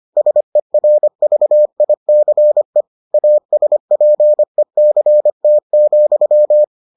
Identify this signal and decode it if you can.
Morse code, heard as a keyed tone, about 25 wpm, 600 Hz. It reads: SERVICE ASPECT,